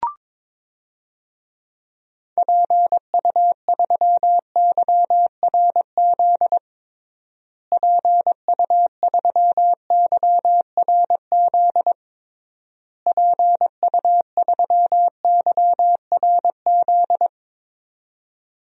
Morse code, heard as a keyed tone, about 22 wpm, 700 Hz. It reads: PU3YRZ PU3YRZ PU3YRZ
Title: DXCC-100 Call Signs - 6 Characters - Repeated 3x - 22wpm